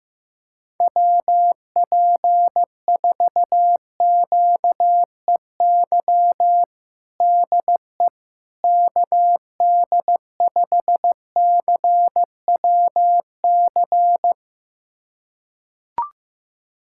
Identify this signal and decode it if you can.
Morse code, heard as a keyed tone, about 15 wpm, 700 Hz. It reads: WP4QEY DE KD5CWC
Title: USA Callsign DE Callsign 15wpm